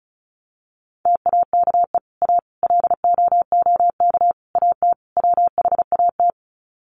Morse code, heard as a keyed tone, about 34 wpm, 700 Hz. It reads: TAKE A LOOK AT WHAT
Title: Sets of 5 Words Encore2 35wpm